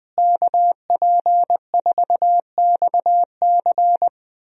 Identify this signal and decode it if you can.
Morse code, heard as a keyed tone, about 21 wpm, 700 Hz. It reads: KP4XC
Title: DXCC-100 Call Signs - 5 Characters - Repeated 3x - 20wpm